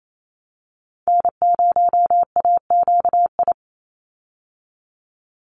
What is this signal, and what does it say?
Morse code, heard as a keyed tone, about 28 wpm, 700 Hz. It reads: N0AQI
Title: USA Call Signs 28wpm